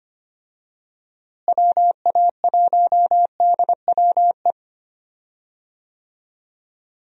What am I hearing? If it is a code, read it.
Morse code, heard as a keyed tone, about 25 wpm, 700 Hz. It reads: WA1DWE
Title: USA Call Signs - 6 Characters - Repeated 3x - 25wpm